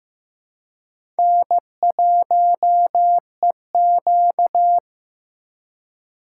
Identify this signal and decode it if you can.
Morse code, heard as a keyed tone, about 15 wpm, 700 Hz. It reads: N1EQ
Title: USA Call Signs - 4 Characters - 2x Character Spacing - 15wpm